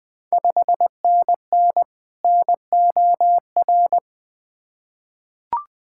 Morse code, heard as a keyed tone, about 20 wpm, 700 Hz. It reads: 5NN NOR
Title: Ontario QSO Party Exchange 20wpm